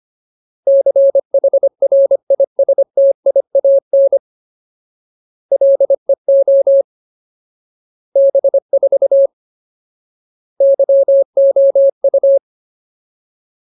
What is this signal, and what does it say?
Morse code, heard as a keyed tone, about 25 wpm, 550 Hz. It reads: CHRISTIAN LEO B4 YOU
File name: Ultimate Word List / 25WPM Ultimate